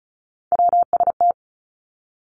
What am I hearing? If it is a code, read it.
Morse code, heard as a keyed tone, about 36 wpm, 700 Hz. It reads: WST
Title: Three Challenging Letter-Number Combinations 35wpm